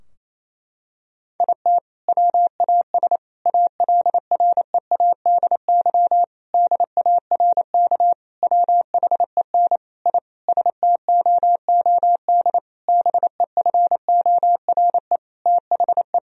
Morse code, heard as a keyed tone, about 28 wpm, 700 Hz. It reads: IT WAS ALREADY DARK WHEN I STOOD BEFORE THE
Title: Sentences from Top 1000 Words - Review Part 2 - 28wpm